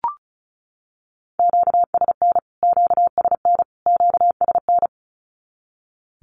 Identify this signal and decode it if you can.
Morse code, heard as a keyed tone, about 34 wpm, 700 Hz. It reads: QSN QSN QSN
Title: Q-Codes Repeated 3x 35wpm